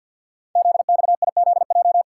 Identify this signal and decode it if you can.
Morse code, heard as a keyed tone, about 50 wpm, 690 Hz. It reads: ZXIZJ
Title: Code Groups - 5 Letters - 50wpm